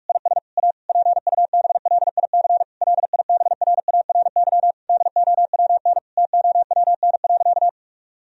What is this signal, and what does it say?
Morse code, heard as a keyed tone, about 46 wpm, 700 Hz. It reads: IS A PUBLIC LIBRARY DOWN TOWN.
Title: Sentences from Top 1000 Words - Review Part 4 - 45wpm